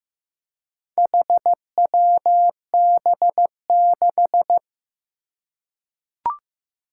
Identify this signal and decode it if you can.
Morse code, heard as a keyed tone, about 15 wpm, 700 Hz. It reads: HWB6
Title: Four Challenging Letter-Number Combinations 15wpm